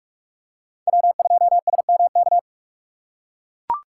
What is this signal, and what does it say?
Morse code, heard as a keyed tone, about 41 wpm, 700 Hz. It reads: W2SMK
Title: USA Call Signs - 5 Characters - 6x Character Spacing - 45wpm